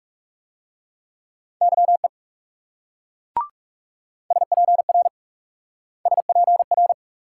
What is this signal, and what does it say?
Morse code, heard as a keyed tone, about 40 wpm, 700 Hz. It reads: YE SPR SPR